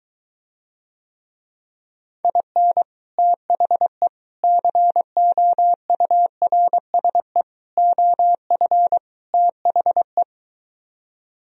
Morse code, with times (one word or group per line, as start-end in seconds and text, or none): 2.2-2.8 IN
3.2-4.1 THE
4.4-7.4 COURSE
7.8-9.0 OF
9.3-10.2 THE